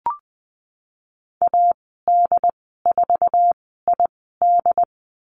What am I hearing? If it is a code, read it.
Morse code, heard as a keyed tone, about 20 wpm, spaced as for 14 wpm, 700 Hz. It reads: AD4ID
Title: 076 Lesson 13g - Letter L - Call Signs 20wpm